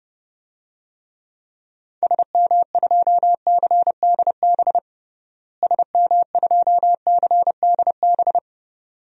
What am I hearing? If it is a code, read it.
Morse code, heard as a keyed tone, about 32 wpm, 700 Hz. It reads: SM2CDB SM2CDB
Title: DXCC-100 Call Signs - 6 Characters - Repeated 2x - 30wpm